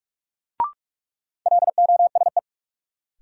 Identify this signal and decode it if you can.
Morse code, heard as a keyed tone, about 46 wpm, 700 Hz. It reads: LOSE
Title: Sentences from Top 1100 Words - New Word Drill - 45wpm